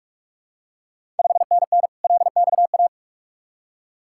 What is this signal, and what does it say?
Morse code, heard as a keyed tone, about 41 wpm, 700 Hz. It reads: 5NN LXA
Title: Ontario QSO Party Exchange 45wpm